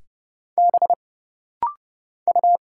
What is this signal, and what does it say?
Morse code, heard as a keyed tone, about 30 wpm, 700 Hz. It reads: B U
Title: Single Letter-Number Alt - Rapid-Fire 30wpm